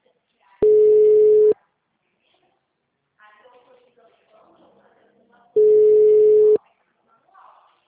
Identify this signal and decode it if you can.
Ringback tone (Europe, 425 Hz), with speech.
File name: out- -2031- - - .86181